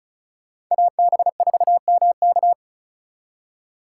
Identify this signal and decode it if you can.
Morse code, heard as a keyed tone, about 42 wpm, 700 Hz. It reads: AB4MK
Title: Speed Racing - USA Call Signs 35wpm